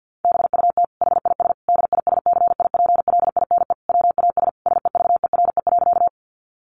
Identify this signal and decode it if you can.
Morse code, heard as a keyed tone, about 50 wpm, 700 Hz. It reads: BUT HIS DISCIPLINE WAS SEVERE.
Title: Sentences from Top 1400 Words 50wpm